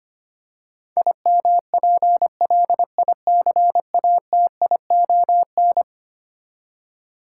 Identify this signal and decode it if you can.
Morse code, heard as a keyed tone, about 25 wpm, 700 Hz. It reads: IMPLICATION
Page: morse-code-ninja-practice-25wpm